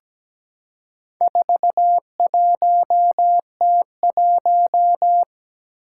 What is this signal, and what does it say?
Morse code, heard as a keyed tone, about 17 wpm, 700 Hz. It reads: 41T1
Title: Four Challenging Letter-Number Combinations 17wpm